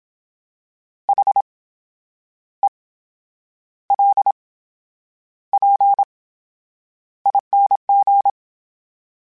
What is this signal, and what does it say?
Morse code, heard as a keyed tone, about 27 wpm, spaced as for 9 wpm, 800 Hz. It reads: HELPING
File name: Special INGs / Special INGs-mp3s / VerySlowToFaster Ing @25